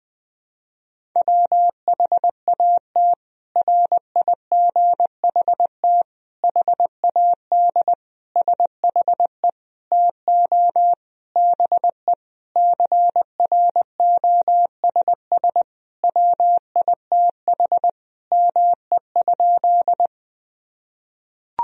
Morse code, heard as a keyed tone, about 21 wpm, 700 Hz. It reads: WHAT RIGHT HAD SHE TO BE CROSS WITH ME?
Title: Sentences from Top 1000 Words - Review Part 1 - 20wpm